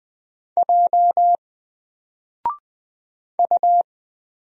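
Morse code, heard as a keyed tone, about 20 wpm, 700 Hz. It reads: J U
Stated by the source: Single letter and number Morse drill at 20 wpm, from N4LCD.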